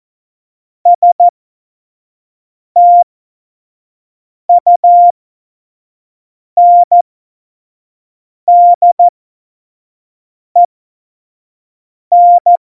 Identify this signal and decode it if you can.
Morse code, heard as a keyed tone, about 14 wpm, spaced as for 5 wpm, 700 Hz. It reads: STUNDEN
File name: Morsekurs / Witze / Witz 19. September 17, Tempo 25